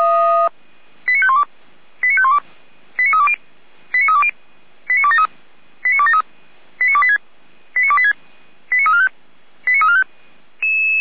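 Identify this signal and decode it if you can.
Five-tone selective call (ZVEI), heard as quick runs of five tones, tones of about 70 ms. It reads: 89512 89512 89320 89320 89273 89273 89286 89286 89345 89345
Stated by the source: Received by SDR, ZVEI five-tone selective call.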